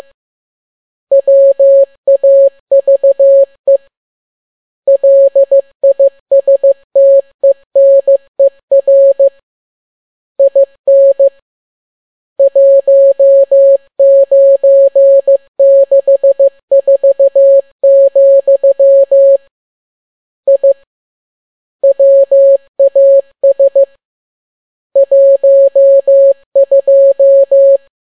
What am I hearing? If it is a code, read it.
Morse code, heard as a keyed tone, about 15 wpm, 550 Hz. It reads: WAVE LISTENER IN 1964, I WAS 12
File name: n6tt p1 15C13